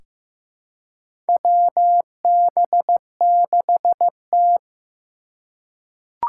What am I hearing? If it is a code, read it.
Morse code, heard as a keyed tone, about 15 wpm, 700 Hz. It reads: WB6T
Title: USA Call Signs - 3x Character Spacing - 15wpm